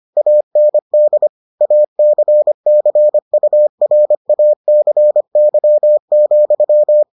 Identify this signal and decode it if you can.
Morse code, heard as a keyed tone, about 25 wpm, 600 Hz. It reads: AND ACCURACY,